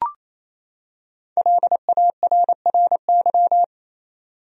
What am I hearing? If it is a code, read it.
Morse code, heard as a keyed tone, about 28 wpm, 700 Hz. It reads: LARRY
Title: USA Common Names - 28wpm